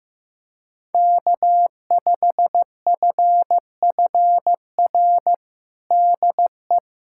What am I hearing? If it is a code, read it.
Morse code, heard as a keyed tone, about 15 wpm, 700 Hz. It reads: K5FFR DE
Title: USA Callsign DE Callsign - 5 Characters - 15wpm